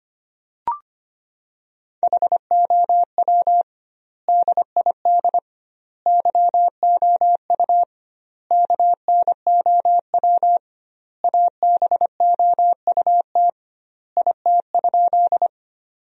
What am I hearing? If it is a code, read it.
Morse code, heard as a keyed tone, about 25 wpm, 700 Hz. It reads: HOW DID YOU KNOW ABOUT IT?
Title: 2x Word-Spacing - Sentences from Top 100 Words - 25wpm